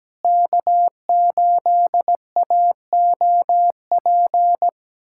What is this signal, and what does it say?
Morse code, heard as a keyed tone, about 17 wpm, 700 Hz. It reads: K8AOP
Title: USA Callsign DE Callsign - 5 Characters - 17wpm